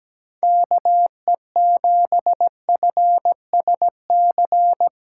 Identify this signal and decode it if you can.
Morse code, heard as a keyed tone, about 17 wpm, 700 Hz. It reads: KE7FSC
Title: USA Callsign DE Callsign - 6 Characters - 17wpm